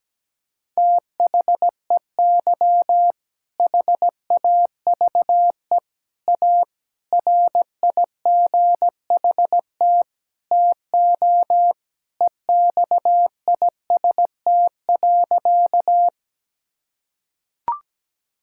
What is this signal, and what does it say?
Morse code, heard as a keyed tone, about 17 wpm, 700 Hz. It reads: THEY HAVE A RIGHT TO EXIST.
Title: Sentences from Top 1000 Words - Review Part 5 - 17wpm